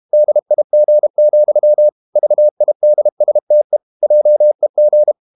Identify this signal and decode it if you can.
Morse code, heard as a keyed tone, about 32 wpm, 600 Hz. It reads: DIG, VIDSTE JEG
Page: Ferie 32 wpm